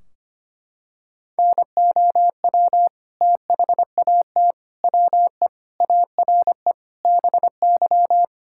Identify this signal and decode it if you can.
Morse code, heard as a keyed tone, about 25 wpm, 700 Hz. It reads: NOW THAT WE ARE BY
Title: Sentences from Top 1600 Words 25wpm